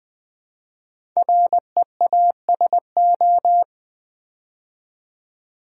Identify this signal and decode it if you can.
Morse code, heard as a keyed tone, about 20 wpm, 700 Hz. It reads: REASO
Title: Build A Word - 5 Letters At A Time - 10-Letter Words - 20wpm